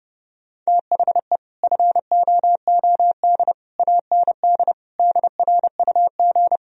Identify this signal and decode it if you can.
Morse code, heard as a keyed tone, about 30 wpm, 700 Hz. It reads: THE FOOD AND DRUG